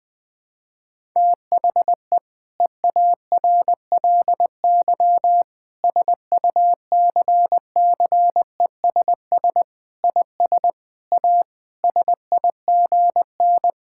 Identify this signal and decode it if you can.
Morse code, heard as a keyed tone, about 21 wpm, 700 Hz. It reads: THE EARLY SUCCESS IS A SIGN
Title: Sentences with 6-Word Phrases - 20wpm